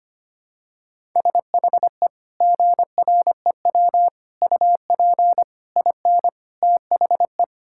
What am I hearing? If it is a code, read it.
Morse code, heard as a keyed tone, about 25 wpm, 700 Hz. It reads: SHE GREW UP IN THE
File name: practice / 25wpm / Sentences with 2-Word Phrases - 25wpm